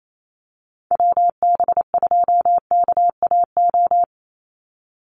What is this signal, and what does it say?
Morse code, heard as a keyed tone, about 28 wpm, 700 Hz. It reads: WB2KAO
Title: Call Signs Repeated 3x - 28wpm